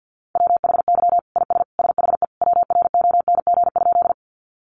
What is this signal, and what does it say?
Morse code, heard as a keyed tone, about 50 wpm, 700 Hz. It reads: WHY IS SHE WRONG?
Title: Sentences from Top 900 Words 50wpm